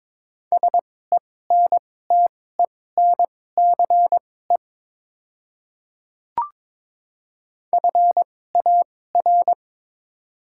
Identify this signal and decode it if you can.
Morse code, heard as a keyed tone, about 22 wpm, spaced as for 16 wpm, 700 Hz. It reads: SENTENCE FAR
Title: 2x Character Spacing - Top 500 Words - 22wpm